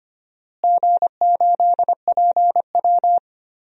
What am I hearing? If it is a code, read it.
Morse code, heard as a keyed tone, about 25 wpm, 700 Hz. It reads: G8PW
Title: DXCC-100 Call Signs - 4 Characters - Repeated 3x - 25wpm